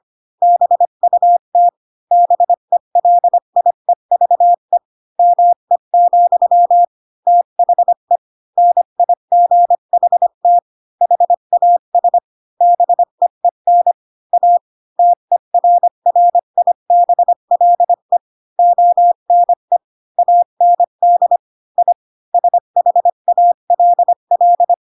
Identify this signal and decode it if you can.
Morse code, heard as a keyed tone, about 25 wpm, 700 Hz. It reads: BUT BELIEVE ME, THE NIGHT HAS BEEN A TERRIBLE ONE AND I SHALL
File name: Pinocchio / Pinocchio ch7 20 25 ik1hge